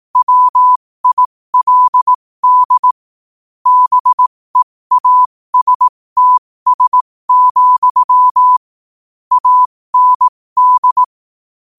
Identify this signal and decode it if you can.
Morse code, heard as a keyed tone, about 18 wpm, spaced as for 15 wpm, 1000 Hz. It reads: WILD BEASTS, AND
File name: qotdfiles / 15wpm